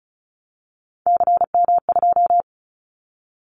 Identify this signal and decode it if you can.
Morse code, heard as a keyed tone, about 34 wpm, 700 Hz. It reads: CM2
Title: Callsign Prefixes for Top 100 DXCC - 35wpm